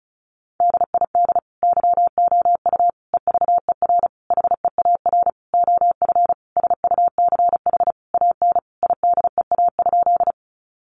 Morse code, heard as a keyed tone, about 34 wpm, 700 Hz. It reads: DID YOU EVER HEAR OF SUCH AN IDEA?
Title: Sentences from Top 400 Words 35wpm